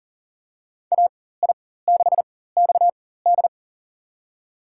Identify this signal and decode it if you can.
Morse code, heard as a keyed tone, about 40 wpm, spaced as for 19 wpm, 700 Hz. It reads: AI6XD
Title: USA Call Signs - 5 Characters - 4x Character Spacing - 40wpm